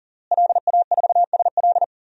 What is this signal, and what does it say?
Morse code, heard as a keyed tone, about 37 wpm, 700 Hz. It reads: LA4SL
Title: DXCC-100 Call Signs - 5 Characters - Repeated 3x - 40wpm